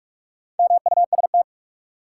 Morse code, heard as a keyed tone, about 46 wpm, 700 Hz. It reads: MUST
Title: Sentences from Top 200 Words - New Word Drill - 45wpm